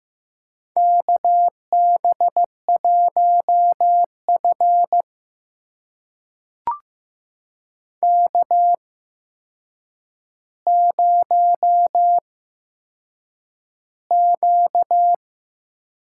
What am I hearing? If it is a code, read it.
Morse code, heard as a keyed tone, about 15 wpm, spaced as for 7 wpm, 700 Hz. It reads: KB1F K0Q
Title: USA Call Signs - 4 Characters - 8x Character Spacing - 15wpm